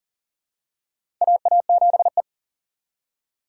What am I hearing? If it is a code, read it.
Morse code, heard as a keyed tone, about 40 wpm, 700 Hz. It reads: AA7E
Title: USA Call Signs - 4 Characters - 3x Character Spacing - 40wpm